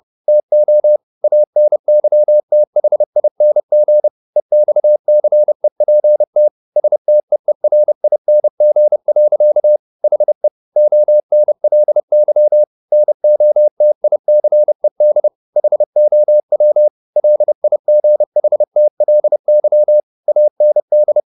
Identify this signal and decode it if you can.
Morse code, heard as a keyed tone, about 30 wpm, 600 Hz. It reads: TO ANYTHING EXCEPT STEERING. HE ONLY NOTICED HOW LIGHTLY AND